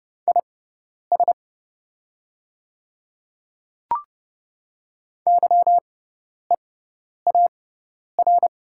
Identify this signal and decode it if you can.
Morse code, heard as a keyed tone, about 30 wpm, spaced as for 10 wpm, 700 Hz. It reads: IS YEAR
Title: 6x Character Spacing - Top 300 Words - 30wpm